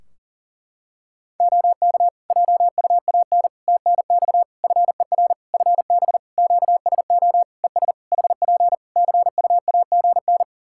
Morse code, heard as a keyed tone, about 40 wpm, 700 Hz. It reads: OK JUAN TNX FER FB QSO ES HP CUAGN
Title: Practice QSO - Exchange 4 - Ending - 40wpm